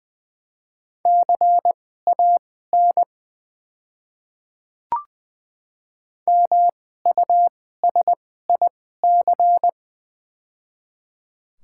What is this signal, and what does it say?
Morse code, heard as a keyed tone, about 20 wpm, spaced as for 15 wpm, 700 Hz. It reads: CAN MUSIC
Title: 2x Character Spacing - Top 300 Words - 20wpm